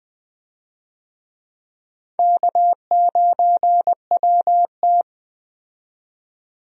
Morse code, heard as a keyed tone, about 21 wpm, 700 Hz. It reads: K9WT
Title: USA Call Signs - 4 Characters - 20wpm